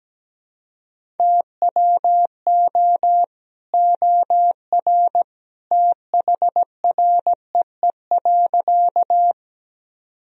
Morse code, heard as a keyed tone, about 17 wpm, 700 Hz. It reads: TWO OR THREE.
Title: Sentences from Top 1000 Words - Review Part 1 - 17wpm